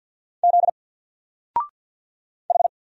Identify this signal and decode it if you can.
Morse code, heard as a keyed tone, about 50 wpm, 700 Hz. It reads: Z H
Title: Single Letter-Number 50wpm